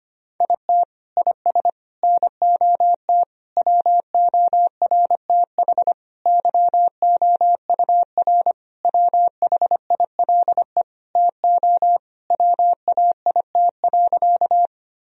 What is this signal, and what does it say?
Morse code, heard as a keyed tone, about 25 wpm, 700 Hz. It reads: IT IS NOT WORTH YOUR WHILE TO WAIT.
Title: Sentences from Top 1000 Words - Review Part 1 - 25wpm